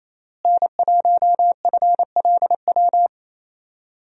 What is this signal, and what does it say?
Morse code, heard as a keyed tone, about 27 wpm, 700 Hz. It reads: N1FLW
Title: USA Call Signs - 5 Characters - Repeated 3x - 28wpm